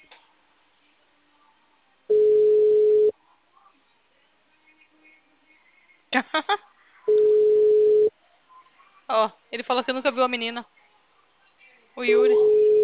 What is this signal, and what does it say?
Ringback tone (Europe, 425 Hz), with speech.
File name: out- -2004- - - .70